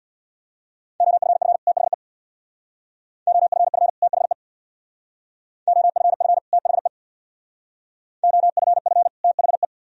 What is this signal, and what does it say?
Morse code, heard as a keyed tone, about 61 wpm, 700 Hz. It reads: OFF THE OFF THE OFF THE OFF THE
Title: Speed-Racing - Sets of 2 Words 50wpm